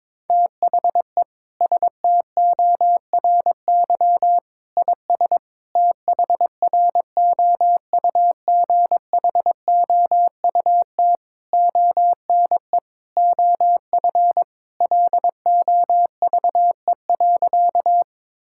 Morse code, heard as a keyed tone, about 21 wpm, 700 Hz. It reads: THE STORY IS THROUGHOUT ONE OF LOVE.
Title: Sentences from Top 1000 Words - Review Part 5 - 22wpm